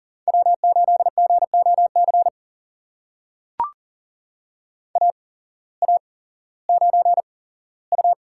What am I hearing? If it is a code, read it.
Morse code, heard as a keyed tone, about 40 wpm, spaced as for 19 wpm, 700 Hz. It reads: W8GOC AA9U
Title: USA Call Signs - 5 Characters - 8x Character Spacing - 40wpm